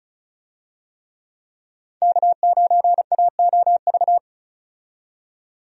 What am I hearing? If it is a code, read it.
Morse code, heard as a keyed tone, about 36 wpm, 700 Hz. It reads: K9AOV
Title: USA Call Signs - 5 Characters - 35wpm